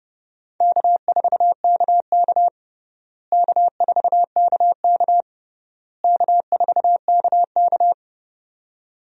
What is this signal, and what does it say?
Morse code, heard as a keyed tone, about 32 wpm, 700 Hz. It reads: K4KK K4KK K4KK
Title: USA Call Signs - 4 Characters - Repeated 3x - 30wpm